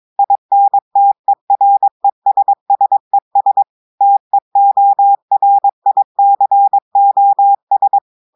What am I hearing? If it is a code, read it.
Morse code, heard as a keyed tone, about 22 wpm, 800 Hz. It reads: INTERESSES TEORICOS
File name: audios / portugais / portugais texte en clair 1 vitesse 22 mots minutes